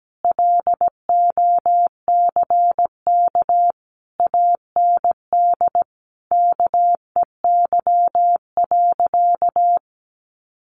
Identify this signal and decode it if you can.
Morse code, heard as a keyed tone, about 17 wpm, 700 Hz. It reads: LOCK AND KEY.
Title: Sentences from Top 1400 Words 17wpm